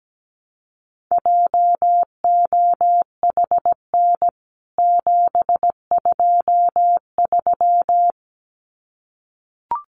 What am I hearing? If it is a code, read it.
Morse code, heard as a keyed tone, about 17 wpm, 700 Hz. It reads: JOHN 723
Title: CWT Exchange 17wpm